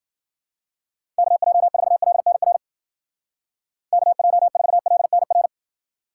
Morse code, heard as a keyed tone, about 48 wpm, 700 Hz. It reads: KJ4LNR KJ4LNR
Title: Speed Racing - USA 2x3 Call Signs 40wpm